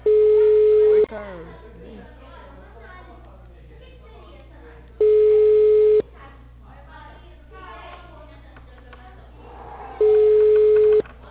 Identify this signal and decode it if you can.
Ringback tone (Europe, 425 Hz), with speech.
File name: out- -2012- - - .74571